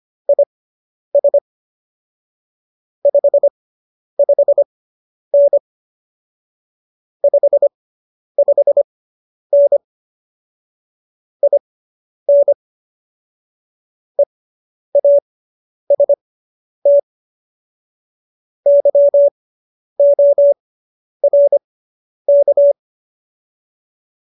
Morse code, heard as a keyed tone, about 25 wpm, spaced as for 10 wpm, 570 Hz. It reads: IS 55N 55N IN EAST YORK